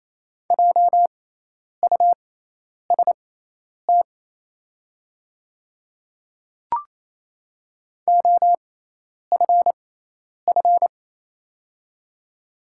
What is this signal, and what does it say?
Morse code, heard as a keyed tone, about 28 wpm, spaced as for 10 wpm, 690 Hz. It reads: JUST OFF
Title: 6x Character Spacing - Top 500 Words - 28wpm